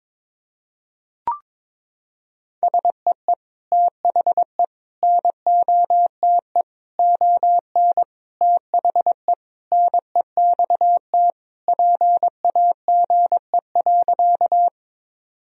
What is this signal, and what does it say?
Morse code, heard as a keyed tone, about 22 wpm, 700 Hz. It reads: SEE THE NOTE ON THE NEXT PAGE.
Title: Sentences from Top 1000 Words - Review Part 2 - 22wpm